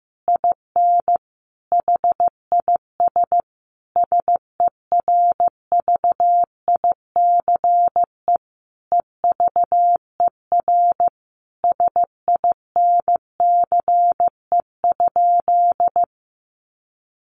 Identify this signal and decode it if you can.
Morse code, heard as a keyed tone, about 15 wpm, 700 Hz. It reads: IN HIS SERVICE EVER SINCE?
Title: Sentences from Top 700 Words 15wpm